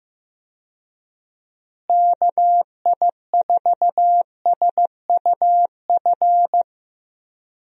Morse code, heard as a keyed tone, about 15 wpm, 700 Hz. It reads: KI4SUF
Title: USA Call Signs - 6 Characters - Repeated 2x - 15wpm